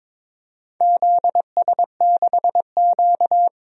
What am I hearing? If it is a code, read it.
Morse code, heard as a keyed tone, about 22 wpm, 700 Hz. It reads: ZS6Q
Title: DXCC-100 Call Signs - 4 Characters - 22wpm